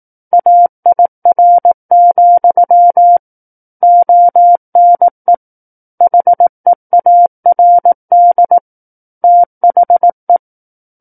Morse code, heard as a keyed tone, about 18 wpm, 700 Hz. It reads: AIR, ONE HEARD THE
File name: Book One / 09-Chapter 9 - 18 WPM